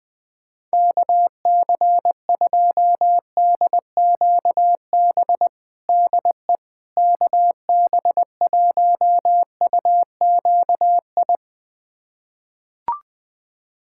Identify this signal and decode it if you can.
Morse code, heard as a keyed tone, about 20 wpm, 700 Hz. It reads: KC2DQB DE KB1UQI
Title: USA Callsign DE Callsign 20wpm